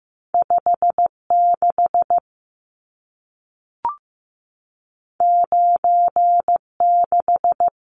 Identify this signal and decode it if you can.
Morse code, heard as a keyed tone, about 15 wpm, 700 Hz. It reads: 56 96
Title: Numbers - 2-Digit 15wpm